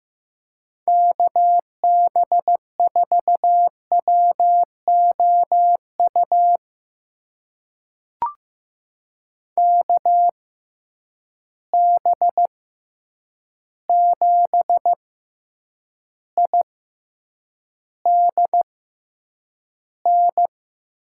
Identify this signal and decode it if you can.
Morse code, heard as a keyed tone, about 15 wpm, spaced as for 8 wpm, 700 Hz. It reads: KB4WOU KB7IDN
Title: USA Call Signs - 6 Characters - 6x Character Spacing - 15wpm